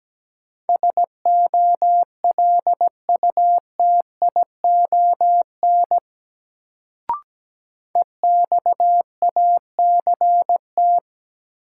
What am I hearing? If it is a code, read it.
Morse code, heard as a keyed tone, about 17 wpm, 700 Hz. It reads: SOLUTION EXACT
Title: Sentences from Top 1300 Words - New Word Drill - 17wpm